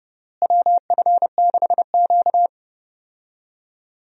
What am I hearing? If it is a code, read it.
Morse code, heard as a keyed tone, about 32 wpm, 700 Hz. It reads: WF6Q
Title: USA Call Signs - 4 Characters - Repeated 3x - 30wpm